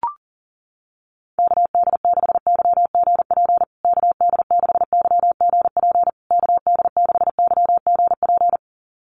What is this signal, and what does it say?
Morse code, heard as a keyed tone, about 40 wpm, 700 Hz. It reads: KD6YGP KD6YGP KD6YGP
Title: USA Call Signs Repeated 3x 40wpm